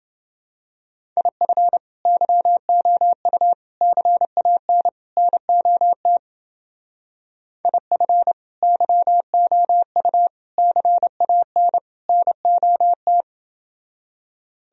Morse code, heard as a keyed tone, about 28 wpm, 700 Hz. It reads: IF YOU CAN NOT IF YOU CAN NOT